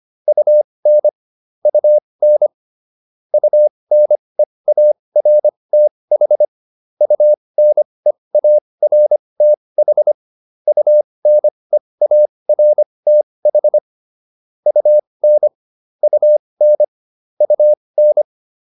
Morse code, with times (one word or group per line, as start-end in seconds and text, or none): 0.3-1.1 UN
1.7-2.5 UN
3.3-6.5 UNEARTH
7.0-10.1 UNEARTH
10.7-13.8 UNEARTH
14.7-15.5 UN
16.0-16.8 UN
17.4-18.2 UN